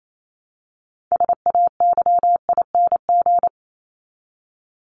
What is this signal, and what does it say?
Morse code, heard as a keyed tone, about 28 wpm, 700 Hz. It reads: SAYING
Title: ING Suffix - Top 100 Words with Base Words 28wpm